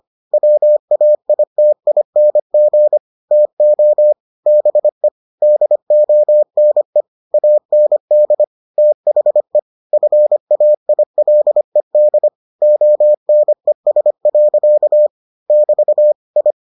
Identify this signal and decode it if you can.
Morse code, heard as a keyed tone, about 25 wpm, 600 Hz. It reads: WAITING TO BE DONE AND THE FAILED ONES. <BT> I